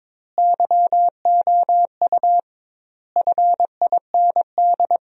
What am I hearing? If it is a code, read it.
Morse code, heard as a keyed tone, about 22 wpm, 700 Hz. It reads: YOU FIND
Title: 2x Word-Spacing - Sentences from Top 200 Words - 22wpm